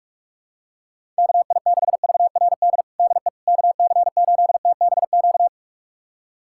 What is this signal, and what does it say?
Morse code, heard as a keyed tone, about 46 wpm, 700 Hz. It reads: KI6VRD DE KC8TBQ
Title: USA Callsign DE Callsign 45wpm